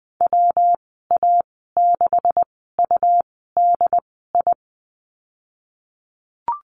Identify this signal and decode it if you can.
Morse code, heard as a keyed tone, about 20 wpm, spaced as for 15 wpm, 700 Hz. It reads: WA6UDI
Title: 193 Lesson 31g - Letter v - Call Signs 20wpm